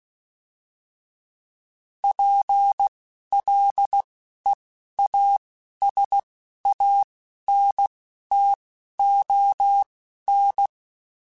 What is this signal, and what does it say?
Morse code, heard as a keyed tone, about 16 wpm, spaced as for 11 wpm, 800 Hz. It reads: PLEASANTON